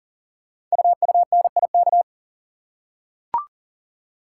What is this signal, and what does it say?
Morse code, heard as a keyed tone, about 43 wpm, 700 Hz. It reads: UUNIK
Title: Code Groups - 5 Letters - 40wpm